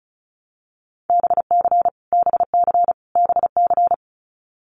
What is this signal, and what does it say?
Morse code, heard as a keyed tone, about 34 wpm, 700 Hz. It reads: BC BC BC
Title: Canadian Provinces and Territories - Repeated 3x - 35wpm